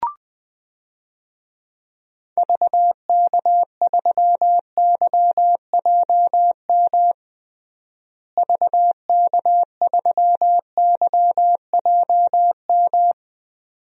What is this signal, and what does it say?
Morse code, heard as a keyed tone, about 20 wpm, 700 Hz. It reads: VK3YJM VK3YJM
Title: DXCC-100 Call Signs - 6 Characters - Repeated 2x - 20wpm